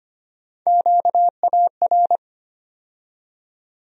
Morse code, heard as a keyed tone, about 25 wpm, 700 Hz. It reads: QAR
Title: USA Call Sign Postfixes - 3 Letters 25wpm